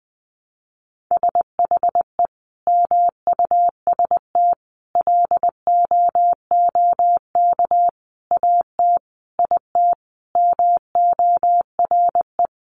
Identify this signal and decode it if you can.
Morse code, heard as a keyed tone, about 20 wpm, 700 Hz. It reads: SHE MUST LOOK AT IT MORE
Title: Sentences from Top 1400 Words 20wpm